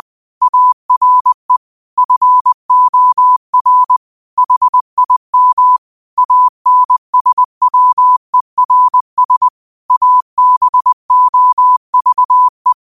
Morse code, heard as a keyed tone, about 20 wpm, 1000 Hz. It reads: ARE FOR HIM ANSWERS ABOVE
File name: qotdfiles / 20wpm